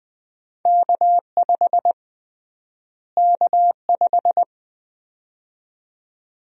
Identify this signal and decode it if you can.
Morse code, heard as a keyed tone, about 20 wpm, 700 Hz. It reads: K5 K5
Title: USA Call Sign Prefixes - Repeated 2x - 20wpm